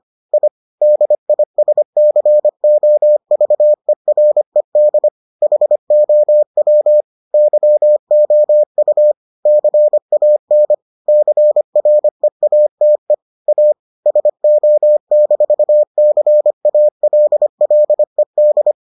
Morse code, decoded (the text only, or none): I DISCOVERED HOW YOU CAN CREATE A SO-CALLED